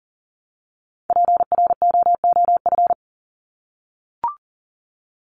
Morse code, heard as a keyed tone, about 40 wpm, 700 Hz. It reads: PROOF
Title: Top 1501-1750 Words 40wpm